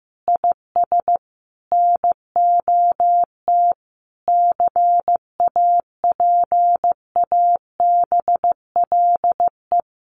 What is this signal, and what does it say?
Morse code, heard as a keyed tone, about 15 wpm, 700 Hz. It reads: IS NOT CAPABLE
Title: Sentences from Top 1200 Words 15wpm